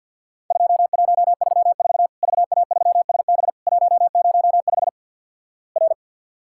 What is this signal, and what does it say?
Morse code, heard as a keyed tone, about 49 wpm, 700 Hz. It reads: 2134 VA3SB 105 R
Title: Worked All Europe DX Contest - QTC Exchange - 50wpm